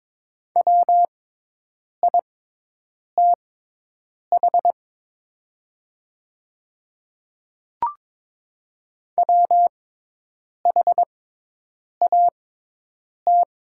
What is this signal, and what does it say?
Morse code, heard as a keyed tone, about 22 wpm, spaced as for 8 wpm, 700 Hz. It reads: WITH WHAT
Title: 6x Character Spacing - Top 200 Words - 22wpm